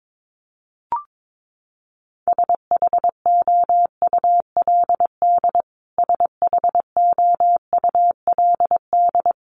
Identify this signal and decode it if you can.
Morse code, heard as a keyed tone, about 22 wpm, 700 Hz. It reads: SHOULD SHOULD
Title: Top 500 Words Repeated 22wpm